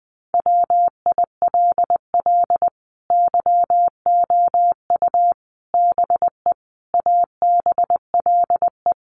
Morse code, heard as a keyed tone, about 19 wpm, 700 Hz. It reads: WILL YOU BE ABLE
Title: Sentences from Top 1100 Words 20wpm